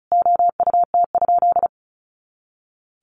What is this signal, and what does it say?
Morse code, heard as a keyed tone, about 34 wpm, 700 Hz. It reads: OUT?
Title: Sentences from Top 300 Words 35wpm